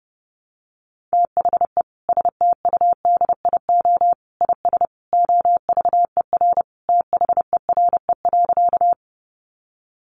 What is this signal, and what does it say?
Morse code, heard as a keyed tone, about 30 wpm, 700 Hz. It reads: THE STUDIO IS OVER THERE.
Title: Sentences from Top 1200 Words 30wpm